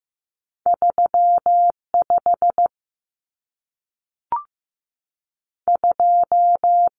Morse code, heard as a keyed tone, about 15 wpm, 700 Hz. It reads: 3 5 2
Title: Numbers - 2-Digit 15wpm